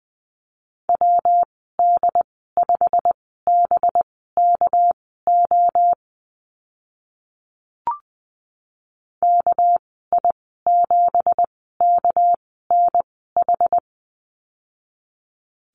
Morse code, heard as a keyed tone, about 20 wpm, spaced as for 14 wpm, 700 Hz. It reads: WD5BKO KI7KNH
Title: 238 Lesson 38g - Letter q - Call Signs 20wpm